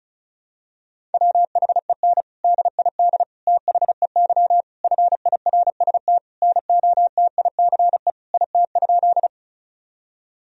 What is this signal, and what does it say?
Morse code, heard as a keyed tone, about 36 wpm, 700 Hz. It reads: WHEN DID THEY FIRST NOTICE IT?
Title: Sentences from Top 1000 Words - Review Part 2 - 35wpm